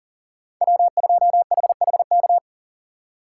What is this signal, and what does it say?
Morse code, heard as a keyed tone, about 39 wpm, 700 Hz. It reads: W2HHK
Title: USA Call Signs - 5 Characters - 4x Character Spacing - 40wpm